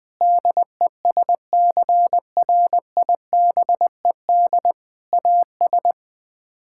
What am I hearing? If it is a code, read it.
Morse code, heard as a keyed tone, about 20 wpm, 700 Hz. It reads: DESCRIBED AS
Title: Sets of 6 Words - 20wpm